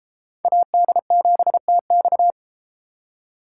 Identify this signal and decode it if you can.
Morse code, heard as a keyed tone, about 32 wpm, 700 Hz. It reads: AD7TX